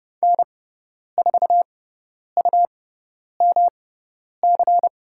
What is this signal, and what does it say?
Morse code, heard as a keyed tone, about 30 wpm, spaced as for 10 wpm, 700 Hz. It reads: N4UMC
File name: Lesson 18g - Letter m - Call Signs 30wpm